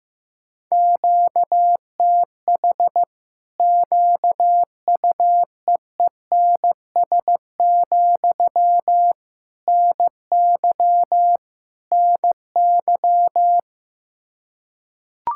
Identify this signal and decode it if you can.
Morse code, heard as a keyed tone, about 15 wpm, 700 Hz. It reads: QTH QUEENS, NY NY
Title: Practice QSO - Exchange 1 - Beginning - 15wpm